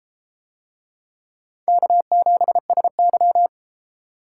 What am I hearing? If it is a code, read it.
Morse code, heard as a keyed tone, about 30 wpm, 700 Hz. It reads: K7SY